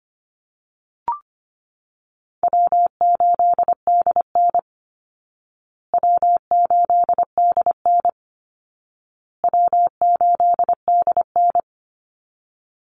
Morse code, heard as a keyed tone, about 25 wpm, 700 Hz. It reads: W8DN W8DN W8DN
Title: Easy Call Signs Repeated 3x - 25wpm